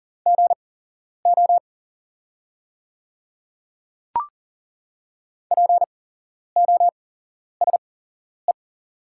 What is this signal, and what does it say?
Morse code, heard as a keyed tone, about 38 wpm, spaced as for 11 wpm, 700 Hz. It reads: GO POSE
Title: 8x Character Spacing - Top 500 Words - 40wpm